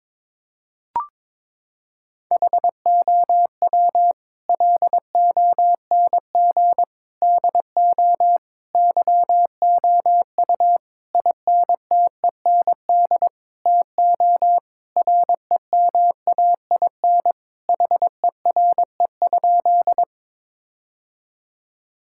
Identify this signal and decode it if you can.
Morse code, heard as a keyed tone, about 22 wpm, 700 Hz. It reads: HOW LONG DO YOU INTEND TO REMAIN HERE?
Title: Sentences from Top 1600 Words 22wpm